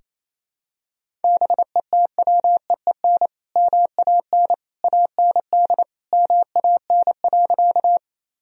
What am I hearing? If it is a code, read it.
Morse code, heard as a keyed tone, about 28 wpm, 700 Hz. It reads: BETWEEN MAN AND MAN.
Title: Sentences from Top 1000 Words - Review Part 1 - 28wpm